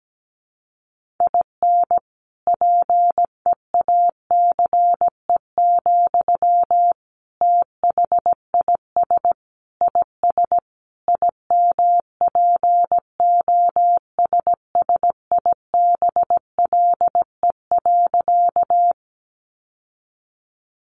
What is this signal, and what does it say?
Morse code, heard as a keyed tone, about 17 wpm, 700 Hz. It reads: IN PEACE, THIS IS IMPOSSIBLE.
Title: Sentences from Top 1500 Words 17wpm